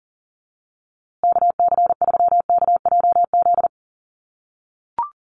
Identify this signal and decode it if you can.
Morse code, heard as a keyed tone, about 40 wpm, 700 Hz. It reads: KC3KJZ
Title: USA Call Signs Repeated 2x 40wpm